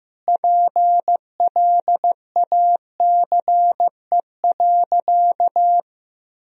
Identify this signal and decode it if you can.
Morse code, heard as a keyed tone, about 15 wpm, 700 Hz. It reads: PLACE.
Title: Sentences from Top 1000 Words - Review Part 3 - 15wpm